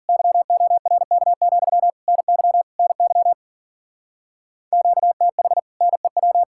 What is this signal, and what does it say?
Morse code, heard as a keyed tone, about 53 wpm, 700 Hz. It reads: YORK, NY NY QTH NEW